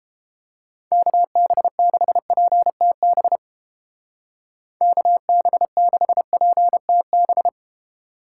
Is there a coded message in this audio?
Yes: Morse code from 0.9 to 7.5 s, heard as a keyed tone, about 29 wpm, 700 Hz. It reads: KB6PTB KB6PTB